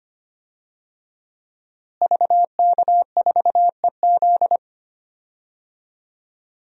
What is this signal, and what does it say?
Morse code, heard as a keyed tone, about 25 wpm, 700 Hz. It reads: VK4EZ
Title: DXCC-100 Call Signs - 5 Characters - Repeated 3x - 25wpm